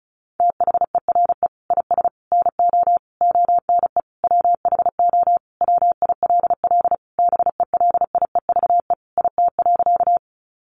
Morse code, heard as a keyed tone, about 34 wpm, 700 Hz. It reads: THERE IS NO ONE WHO WILL BELIEVE IT.
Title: Sentences from Top 1000 Words 35wpm